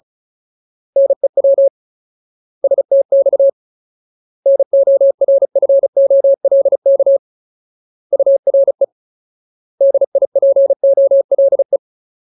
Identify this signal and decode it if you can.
Morse code, heard as a keyed tone, about 34 wpm, 550 Hz. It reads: NEW STX NORFOLK URE DIPOLE
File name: Ultimate Word List / 35WPM Ultimate